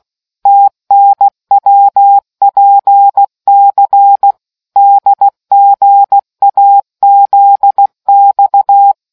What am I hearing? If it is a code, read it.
Morse code, heard as a keyed tone, about 16 wpm, 800 Hz. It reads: TNWPC DGAZX